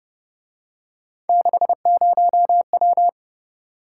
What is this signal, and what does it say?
Morse code, heard as a keyed tone, about 29 wpm, 700 Hz. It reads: 60W
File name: Speed-Racing - Three Letter-Number Combinations - 25wpm